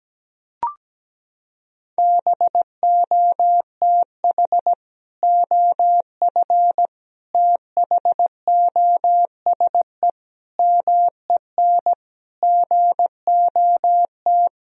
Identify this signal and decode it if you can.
Morse code, heard as a keyed tone, about 17 wpm, 700 Hz. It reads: BOTH OF THOSE MEN GOT
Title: Sentences from Top 1000 Words - Review Part 1 - 17wpm